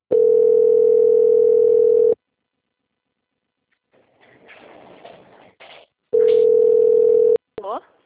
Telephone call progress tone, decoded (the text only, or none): ringback tone